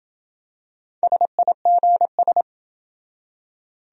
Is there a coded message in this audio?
Yes: Morse code from 1.0 to 2.4 s, heard as a keyed tone, about 26 wpm, 700 Hz. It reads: SIGS